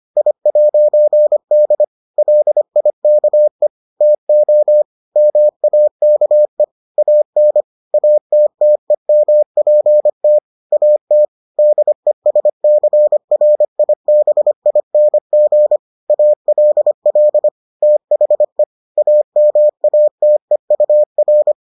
Morse code, heard as a keyed tone, about 25 wpm, 600 Hz. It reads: I'D LIKE TO MAKE AN ATTEMPT AT DESCRIBING ALL THE AMATEUR